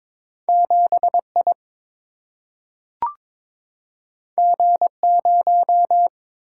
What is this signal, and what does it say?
Morse code, heard as a keyed tone, about 22 wpm, 700 Hz. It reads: 7 I G 0
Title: Single-Letter Single-Number 22wpm